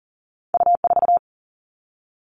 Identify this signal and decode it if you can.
Morse code, heard as a keyed tone, about 42 wpm, 700 Hz. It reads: U4
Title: All Cut Numbers - 2-Digit 40wpm